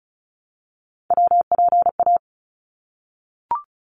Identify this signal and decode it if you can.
Morse code, heard as a keyed tone, about 36 wpm, 700 Hz. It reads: WPA
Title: ARRL Sections 35wpm